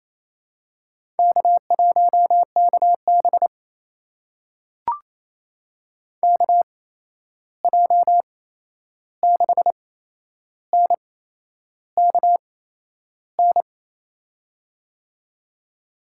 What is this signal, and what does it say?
Morse code, heard as a keyed tone, about 28 wpm, spaced as for 11 wpm, 700 Hz. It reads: K1KB KJ6NKN
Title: USA Call Signs - 8x Character Spacing - 28wpm